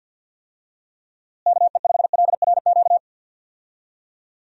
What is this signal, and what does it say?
Morse code, heard as a keyed tone, about 51 wpm, 700 Hz. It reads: KE5LRQ
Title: USA Call Signs - 6 Characters - Repeated 2x - 50wpm